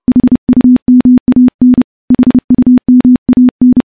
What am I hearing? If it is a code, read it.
Morse code, heard as a keyed tone, about 30 wpm, 250 Hz. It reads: HUMAN HUMAN
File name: morse / morse-training-30WPM